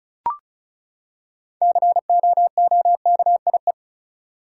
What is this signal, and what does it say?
Morse code, heard as a keyed tone, about 36 wpm, 700 Hz. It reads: COOKIE
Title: Six Letter Words - 35wpm